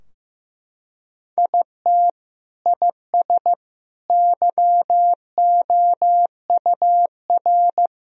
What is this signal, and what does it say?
Morse code, heard as a keyed tone, about 16 wpm, 700 Hz. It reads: IT IS YOUR
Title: Sentences from Top 1000 Words - Review Part 2 - 15wpm